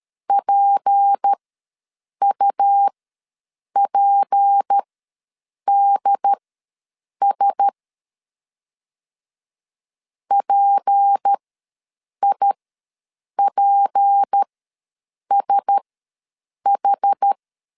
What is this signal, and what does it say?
Morse code, heard as a keyed tone, about 13 wpm, spaced as for 7 wpm, 800 Hz. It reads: PUPDS PIPSH
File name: LEZIONI / 6 PKX0 / 6 P ws 12wpm 3117